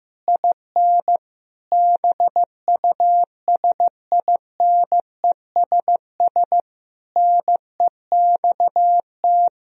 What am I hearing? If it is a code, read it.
Morse code, heard as a keyed tone, about 16 wpm, 700 Hz. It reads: IN BUSINESS NEXT
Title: Sentences from Top 1000 Words - Review Part 4 - 15wpm